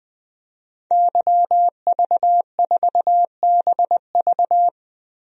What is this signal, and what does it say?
Morse code, heard as a keyed tone, about 20 wpm, 700 Hz. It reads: YV4BV
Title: DXCC-100 Call Signs - 5 Characters - Repeated 3x - 20wpm